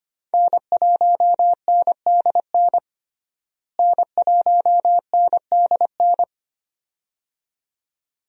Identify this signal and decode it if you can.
Morse code, heard as a keyed tone, about 25 wpm, 700 Hz. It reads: N1NDN N1NDN
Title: USA Call Signs - 5 Characters - Repeated 2x - 25wpm